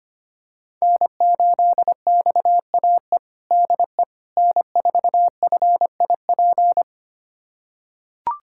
Morse code, heard as a keyed tone, about 25 wpm, 700 Hz. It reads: N8XAE DE N4FIP
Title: USA Callsign DE Callsign - 5 Characters - 25wpm